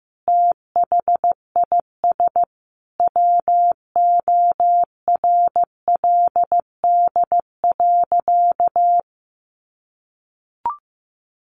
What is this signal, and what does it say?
Morse code, heard as a keyed tone, about 15 wpm, 700 Hz. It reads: THIS WORLD.
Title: Sentences from Top 600 Words 15wpm